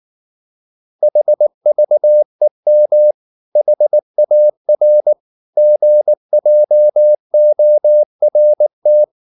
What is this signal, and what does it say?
Morse code, heard as a keyed tone, about 19 wpm, 600 Hz. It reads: HVEM HAR GJORT
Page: Ferie 19 wpm